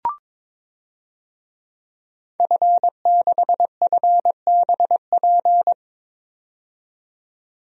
Morse code, heard as a keyed tone, about 22 wpm, 700 Hz. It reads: F6FBP
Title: DXCC-100 Call Signs - 5 Characters - 22wpm